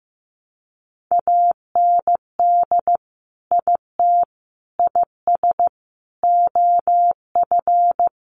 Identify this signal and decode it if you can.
Morse code, heard as a keyed tone, about 15 wpm, 700 Hz. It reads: AND IT IS OF
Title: Sentences from Top 1100 Words 15wpm